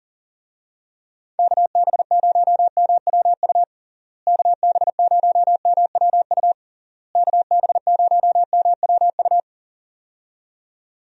Morse code, heard as a keyed tone, about 40 wpm, 700 Hz. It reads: KB0MWU KB0MWU KB0MWU
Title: USA Call Signs - 6 Characters - Repeated 3x - 40wpm